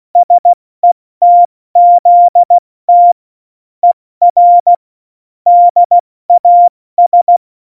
Morse code, heard as a keyed tone, about 16 wpm, 700 Hz. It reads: SETZT ER DAS
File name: Morsekurs / Witze / Witz 11 August 2020, 70 F80